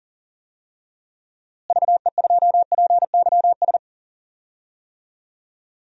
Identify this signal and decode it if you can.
Morse code, heard as a keyed tone, about 37 wpm, 700 Hz. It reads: VE2PYS
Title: DXCC-100 Call Signs - 6 Characters - Repeated 3x - 40wpm